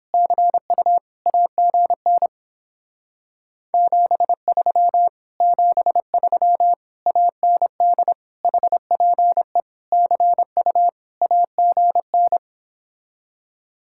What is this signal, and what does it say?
Morse code, heard as a keyed tone, about 29 wpm, 700 Hz. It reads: CU AGN 73 73 AND HPE CU AGN